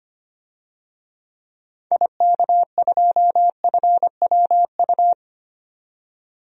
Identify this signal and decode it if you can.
Morse code, heard as a keyed tone, about 25 wpm, 700 Hz. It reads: IK2FWU
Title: DXCC-100 Call Signs - 6 Characters - 25wpm